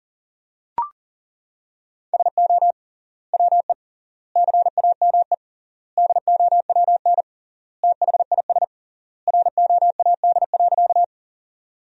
Morse code, heard as a keyed tone, about 40 wpm, 700 Hz. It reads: SO WE CAME DOWN THIS ROAD.
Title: 3x Word-Spacing - Sentences from Top 400 Words - 40wpm